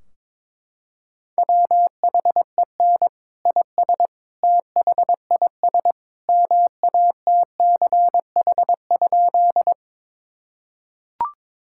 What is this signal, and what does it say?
Morse code, heard as a keyed tone, about 22 wpm, 700 Hz. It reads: WHEN IS THIS MATCH?
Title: Sentences from Top 1600 Words 22wpm